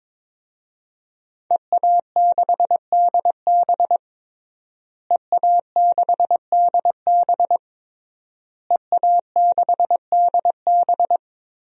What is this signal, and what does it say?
Morse code, heard as a keyed tone, about 22 wpm, 700 Hz. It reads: EA6DB EA6DB EA6DB
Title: DXCC-100 Call Signs - 5 Characters - Repeated 3x - 22wpm